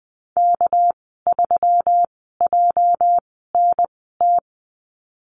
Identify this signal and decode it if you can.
Morse code, heard as a keyed tone, about 20 wpm, spaced as for 15 wpm, 700 Hz. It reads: K3JNT
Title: 207 Lesson 33g - Letter j - Call Signs 20wpm